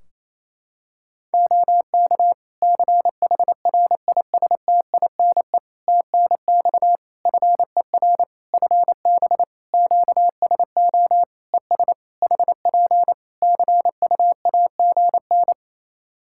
Morse code, heard as a keyed tone, about 28 wpm, 700 Hz. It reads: OK CHRISTINE TNX FER FB QSO ES HP CUAGN
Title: Practice QSO - Exchange 4 - Ending - 28wpm